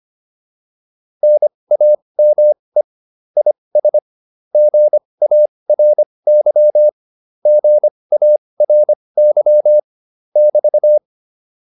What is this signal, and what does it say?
Morse code, heard as a keyed tone, about 25 wpm, spaced as for 20 wpm, 600 Hz. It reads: NAME IS GARY GARY <BT>
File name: QSO103 20